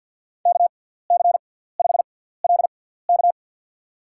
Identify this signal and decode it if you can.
Morse code, heard as a keyed tone, about 52 wpm, spaced as for 17 wpm, 700 Hz. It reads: KC5LK
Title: USA Call Signs - 5 Characters - 6x Character Spacing - 50wpm